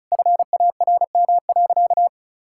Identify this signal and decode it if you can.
Morse code, heard as a keyed tone, about 36 wpm, 700 Hz. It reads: FARM.
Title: Sentences from Top 1000 Words - Review Part 6 - 35wpm